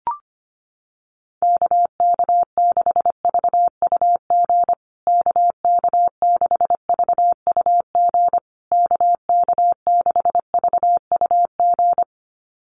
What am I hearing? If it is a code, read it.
Morse code, heard as a keyed tone, about 25 wpm, 700 Hz. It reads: KK6VUG KK6VUG KK6VUG
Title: USA Call Signs Repeated 3x 25wpm